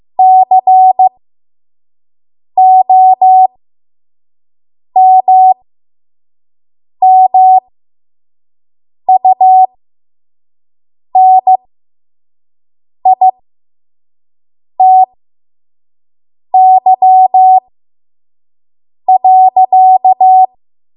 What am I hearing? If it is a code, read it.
Morse code, heard as a keyed tone, about 15 wpm, spaced as for 5 wpm, 750 Hz. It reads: COMMUNITY.
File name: Morse / 05WPM